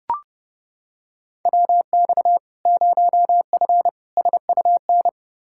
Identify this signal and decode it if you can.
Morse code, heard as a keyed tone, about 30 wpm, 700 Hz. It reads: WX 0F SUN
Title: Practice QSO - Exchange 2 - Info - 30wpm